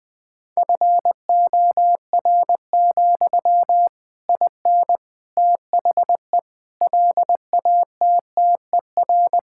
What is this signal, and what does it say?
Morse code, heard as a keyed tone, about 20 wpm, 700 Hz. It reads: FOR, IN THE LATTER
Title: Sentences from Top 1600 Words 20wpm